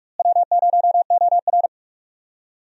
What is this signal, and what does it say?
Morse code, heard as a keyed tone, about 46 wpm, 700 Hz. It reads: W0OR
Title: USA Call Signs - 4 Characters - Repeated 3x - 45wpm